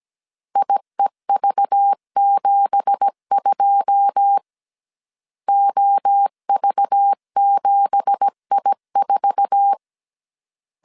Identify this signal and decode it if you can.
Morse code, heard as a keyed tone, about 17 wpm, 800 Hz. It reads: IEV72 OV7I4
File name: CW / DET / 1 9 1500C S A 16wpm 2828